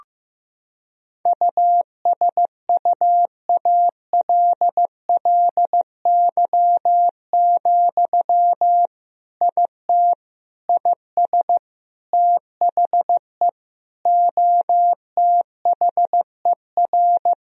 Morse code, heard as a keyed tone, about 15 wpm, 700 Hz. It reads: USUALLY, IT IS THE OTHER
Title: Sentences from Top 1000 Words - Review Part 3 - 15wpm